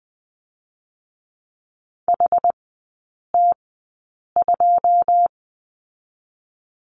Morse code, heard as a keyed tone, about 20 wpm, spaced as for 8 wpm, 700 Hz. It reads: HT2
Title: 107 Lesson 18e - Letter m - Three Characters 20wpm